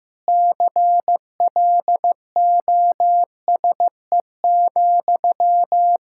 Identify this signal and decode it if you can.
Morse code, heard as a keyed tone, about 15 wpm, 700 Hz. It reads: CLOSE,
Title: Sentences from Top 1600 Words 15wpm